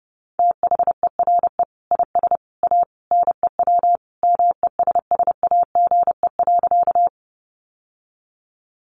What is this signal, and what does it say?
Morse code, heard as a keyed tone, about 30 wpm, 700 Hz. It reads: THERE IS A NEW MESSAGE.
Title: Sentences from Top 1100 Words 30wpm